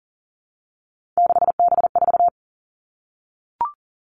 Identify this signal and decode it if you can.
Morse code, heard as a keyed tone, about 40 wpm, 700 Hz. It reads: 6B4
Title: All Cut Numbers - 3-Digit 40wpm